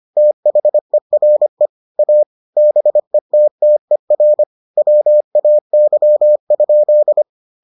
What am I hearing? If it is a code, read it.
Morse code, heard as a keyed tone, about 25 wpm, 600 Hz. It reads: THERE A BETTER WAY?